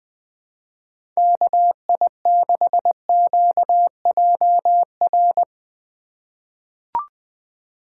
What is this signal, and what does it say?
Morse code, heard as a keyed tone, about 20 wpm, 700 Hz. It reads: KI6QJR
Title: USA Call Signs - 6 Characters - 3x Character Spacing - 20wpm